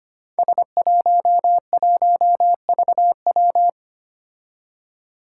Morse code, heard as a keyed tone, about 25 wpm, 700 Hz. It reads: S11VW
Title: Five Challenging Letter-Number Combinations 25wpm